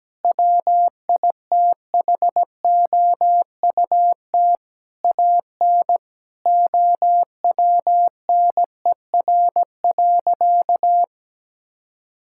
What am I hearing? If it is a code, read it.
Morse code, heard as a keyed tone, about 17 wpm, 700 Hz. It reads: WITHOUT AN OWNER.
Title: Sentences from Top 1600 Words 17wpm